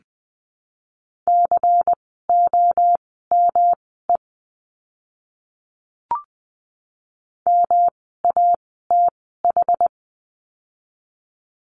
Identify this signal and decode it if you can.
Morse code, heard as a keyed tone, about 20 wpm, spaced as for 14 wpm, 700 Hz. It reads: COME MATH
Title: 108 Lesson 18f - Letter m - Words 20wpm